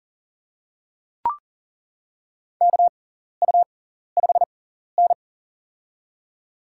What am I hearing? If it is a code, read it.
Morse code, heard as a keyed tone, about 40 wpm, spaced as for 14 wpm, 700 Hz. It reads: KU5N
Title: USA Call Signs - 4 Characters - 6x Character Spacing - 40wpm